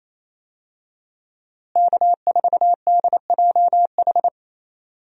Morse code, heard as a keyed tone, about 28 wpm, 700 Hz. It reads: K4DJH
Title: USA Call Signs - 5 Characters - Repeated 2x - 28wpm